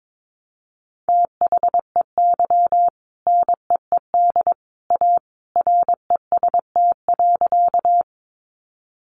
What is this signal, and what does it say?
Morse code, heard as a keyed tone, about 22 wpm, 700 Hz. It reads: THEY NEED A REST.
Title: Sentences from Top 500 Words 22wpm